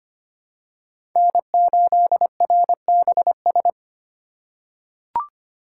Morse code, heard as a keyed tone, about 25 wpm, 700 Hz. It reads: N8RBS
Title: USA Call Signs - 5 Characters - 6x Character Spacing - 25wpm